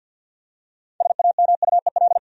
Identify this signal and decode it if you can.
Morse code, heard as a keyed tone, about 52 wpm, 700 Hz. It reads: SAMUEL
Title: USA Common Names - 50wpm